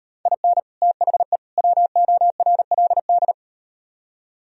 Morse code, heard as a keyed tone, about 37 wpm, 700 Hz. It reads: IN THE WORLD